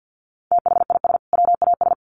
Morse code, heard as a keyed tone, about 50 wpm, 700 Hz. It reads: THIS WAS
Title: Sentences from Top 1000 Words 50wpm